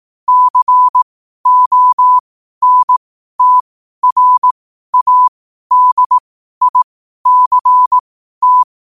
Morse code, heard as a keyed tone, about 18 wpm, spaced as for 12 wpm, 1000 Hz. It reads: CONTRADICT
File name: qotdfiles / 12wpm